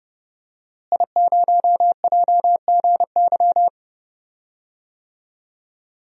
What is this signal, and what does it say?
Morse code, heard as a keyed tone, about 30 wpm, 700 Hz. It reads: I0JGY
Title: DXCC-100 Call Signs - 5 Characters - Repeated 2x - 30wpm